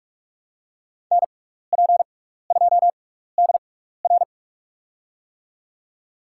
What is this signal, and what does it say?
Morse code, heard as a keyed tone, about 44 wpm, spaced as for 15 wpm, 700 Hz. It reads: NP2DR
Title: USA Call Signs - 5 Characters - 6x Character Spacing - 45wpm